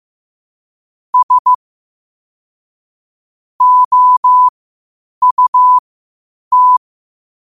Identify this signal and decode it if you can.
Morse code, heard as a keyed tone, about 15 wpm, spaced as for 8 wpm, 1000 Hz. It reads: S OUT
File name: qotdfiles / 8wpm